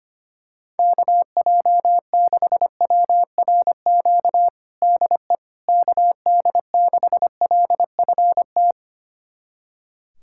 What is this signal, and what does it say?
Morse code, heard as a keyed tone, about 25 wpm, 700 Hz. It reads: KJ6WRQ DE KD6LFT
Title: USA Callsign DE Callsign - 6 Characters - 25wpm